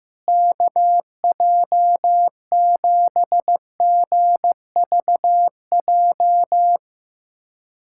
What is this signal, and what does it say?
Morse code, heard as a keyed tone, about 15 wpm, 700 Hz. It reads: KJ7GVJ
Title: USA Callsign DE Callsign 15wpm